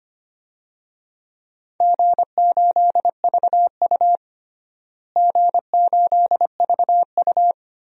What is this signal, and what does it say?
Morse code, heard as a keyed tone, about 25 wpm, 700 Hz. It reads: G8VU G8VU
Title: DXCC-100 Call Signs - 4 Characters - Repeated 3x - 25wpm